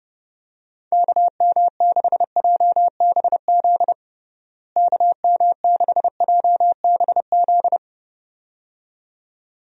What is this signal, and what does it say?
Morse code, heard as a keyed tone, about 32 wpm, 700 Hz. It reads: KM6JBZ KM6JBZ
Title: USA Call Signs - 6 Characters - Repeated 2x - 30wpm